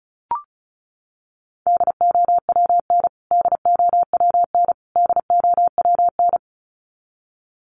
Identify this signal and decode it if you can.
Morse code, heard as a keyed tone, about 34 wpm, 700 Hz. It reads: DOWN DOWN DOWN
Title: Top 200 Words Repeated 35wpm